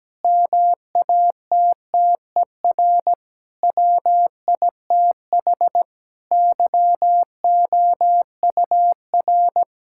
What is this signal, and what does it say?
Morse code, heard as a keyed tone, about 17 wpm, 700 Hz. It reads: MATTER WITH YOUR
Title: Sentences from Top 1000 Words - Review Part 5 - 17wpm